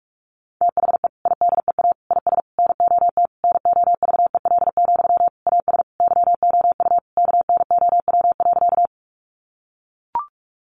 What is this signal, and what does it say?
Morse code, heard as a keyed tone, about 44 wpm, 700 Hz. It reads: THE IDEA IS NOT NOVEL, AS YOU KNOW.
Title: Sentences from Top 1300 Words 45wpm